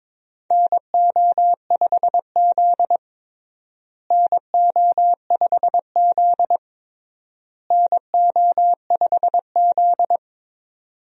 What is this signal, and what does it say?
Morse code, heard as a keyed tone, about 22 wpm, 700 Hz. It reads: NO5Z NO5Z NO5Z
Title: USA Call Signs - 4 Characters - Repeated 3x - 22wpm